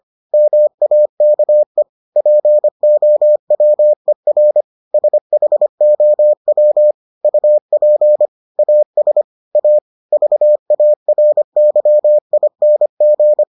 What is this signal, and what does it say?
Morse code, heard as a keyed tone, about 25 wpm, 600 Hz. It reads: MAKE POWER SHOW UP AS A VARYING